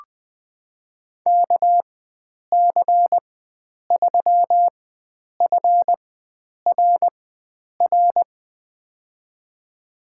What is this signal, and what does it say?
Morse code, heard as a keyed tone, about 20 wpm, spaced as for 9 wpm, 700 Hz. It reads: KC3FRR
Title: USA Call Signs - 6 Characters - 4x Character Spacing - 20wpm